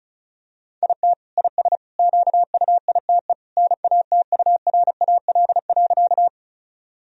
Morse code, heard as a keyed tone, about 34 wpm, 700 Hz. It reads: IT IS QUITE NATURAL.
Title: Sentences from Top 1000 Words - Review Part 4 - 35wpm